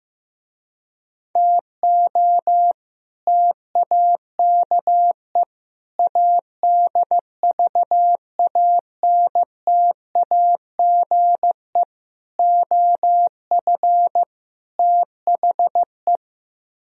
Morse code, heard as a keyed tone, about 15 wpm, 700 Hz. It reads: TO TAKE ADVANTAGE OF THE
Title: Speed-Racing - Sets of 5 Words 15wpm